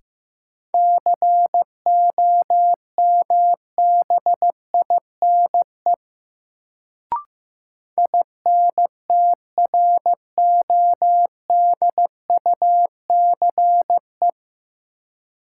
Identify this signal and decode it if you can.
Morse code, heard as a keyed tone, about 15 wpm, 700 Hz. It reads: COMBINE INTRODUCE
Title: Sentences from Top 1600 Words - New Word Drill - 15wpm